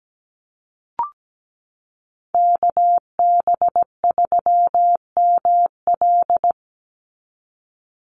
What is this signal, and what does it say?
Morse code, heard as a keyed tone, about 17 wpm, 700 Hz. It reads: KB3ML
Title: Call Signs Repeated 3x - 17wpm